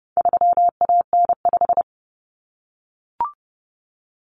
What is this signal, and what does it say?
Morse code, heard as a keyed tone, about 30 wpm, 700 Hz. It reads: 3AN5
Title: Common Cut Numbers - 4-Digit 30wpm